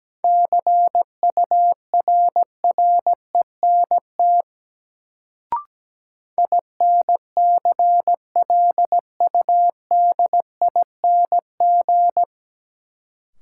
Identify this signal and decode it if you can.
Morse code, heard as a keyed tone, about 17 wpm, 700 Hz. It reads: CURRENT INCLUDING
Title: Sentences from Top 1000 Words - Review Part 4 - New Word Drill - 17wpm